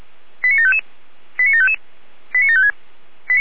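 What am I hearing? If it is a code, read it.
Five-tone selective call (ZVEI), heard as quick runs of five tones, tones of about 70 ms. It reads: 89755 89755 89756
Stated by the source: Five-tone ZVEI selective call received on an SDR.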